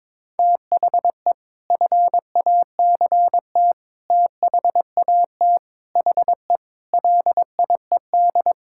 Morse code, heard as a keyed tone, about 22 wpm, 700 Hz. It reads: THE FACT THAT HE LIED
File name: practice / 22wpm / Sentences with 4-Word Phrases - 22wpm